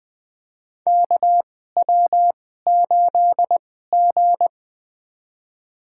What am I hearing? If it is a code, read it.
Morse code, heard as a keyed tone, about 21 wpm, spaced as for 15 wpm, 700 Hz. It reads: KW8G
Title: USA Call Signs - 4 Characters - 2x Character Spacing - 20wpm